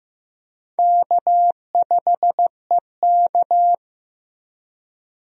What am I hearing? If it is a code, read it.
Morse code, heard as a keyed tone, about 15 wpm, 700 Hz. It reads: K5EK
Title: USA Call Signs - 4 Characters - Repeated 3x - 15wpm